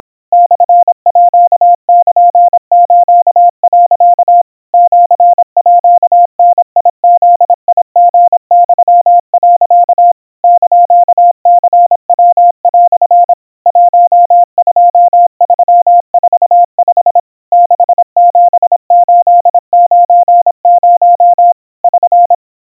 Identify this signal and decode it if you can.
Morse code (Japanese Wabun code), heard as a keyed tone, about 26 wpm, 700 Hz. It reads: モールス、 シーダブリユ、 <KN>ニヤ) 12345 67890 <SN>
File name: aiueo 90 cpm 700 Hz Crispy